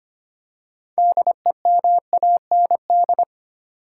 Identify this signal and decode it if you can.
Morse code, heard as a keyed tone, about 25 wpm, 700 Hz. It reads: DEMAND
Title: Six Letter Words - 25wpm